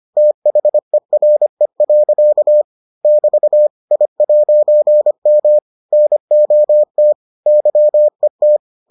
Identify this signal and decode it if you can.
Morse code, heard as a keyed tone, about 25 wpm, 600 Hz. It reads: THERE. <BT> I'M NOT YET